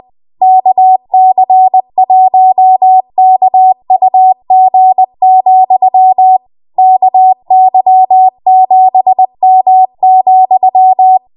Morse code, heard as a keyed tone, about 20 wpm, 750 Hz. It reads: KC1KUG, KY7M,
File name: Morse / 20WPM